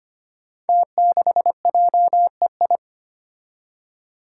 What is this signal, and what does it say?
Morse code, heard as a keyed tone, about 25 wpm, 700 Hz. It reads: T6JEI
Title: Five Challenging Letter-Number Combinations 25wpm